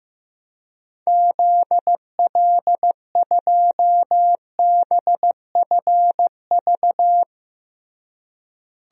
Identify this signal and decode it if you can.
Morse code, heard as a keyed tone, about 16 wpm, 700 Hz. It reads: ZL2BFV
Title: DXCC-100 Call Signs - 6 Characters - Repeated 3x - 15wpm